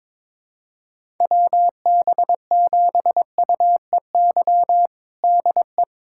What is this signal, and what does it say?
Morse code, heard as a keyed tone, about 22 wpm, 700 Hz. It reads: WB7UEY DE
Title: USA Callsign DE Callsign - 6 Characters - 22wpm